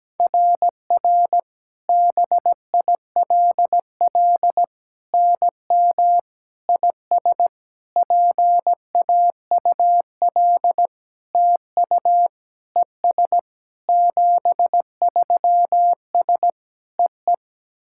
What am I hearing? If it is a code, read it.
Morse code, heard as a keyed tone, about 17 wpm, 700 Hz. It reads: RR BILL NM IS PAUL TU ES 73S EE
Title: SOTA QSO Practice 17wpm